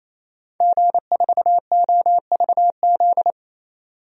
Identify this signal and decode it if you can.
Morse code, heard as a keyed tone, about 28 wpm, 700 Hz. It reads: G4OVZ
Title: DXCC-100 Call Signs - 5 Characters - 28wpm